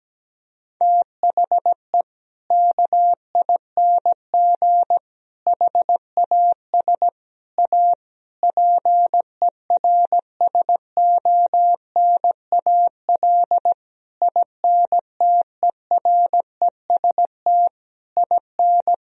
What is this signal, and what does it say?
Morse code, heard as a keyed tone, about 17 wpm, 700 Hz. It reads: THE KING HAS A PERSONAL INTEREST IN
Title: Sentences from Top 1000 Words - Review Part 4 - 17wpm